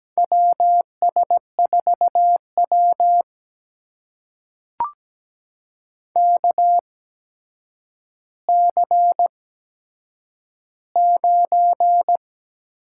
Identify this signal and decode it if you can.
Morse code, heard as a keyed tone, about 17 wpm, spaced as for 8 wpm, 700 Hz. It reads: WS4W KC9
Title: USA Call Signs - 8x Character Spacing - 17wpm